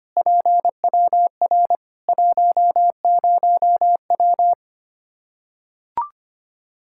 Morse code, heard as a keyed tone, about 25 wpm, 700 Hz. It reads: PWR 10W
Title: K1USN SST QSO Practice 25wpm